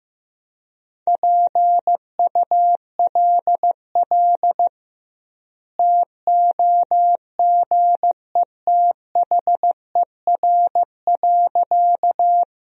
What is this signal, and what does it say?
Morse code, heard as a keyed tone, about 15 wpm, 700 Hz. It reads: PULL TOGETHER.
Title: 2x Word-Spacing - Sentences from Top 400 Words - 15wpm